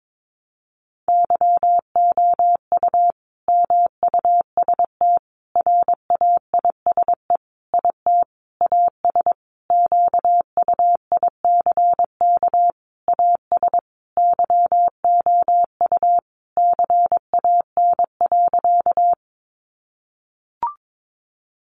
Morse code, heard as a keyed tone, about 22 wpm, 700 Hz. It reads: YOU MUST RAISE IT AS QUICK AS YOU CAN.
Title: Sentences from Top 1400 Words 22wpm